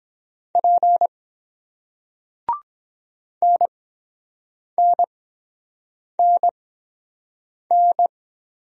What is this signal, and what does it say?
Morse code, heard as a keyed tone, about 24 wpm, spaced as for 7 wpm, 700 Hz. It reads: P NNNN